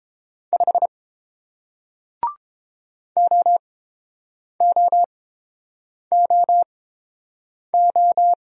Morse code, heard as a keyed tone, about 31 wpm, spaced as for 8 wpm, 700 Hz. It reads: 5 OOOO